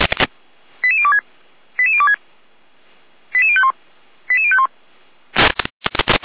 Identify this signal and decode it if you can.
Five-tone selective call (ZVEI), heard as quick runs of five tones, tones of about 70 ms. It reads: 88026 88026 88051 88051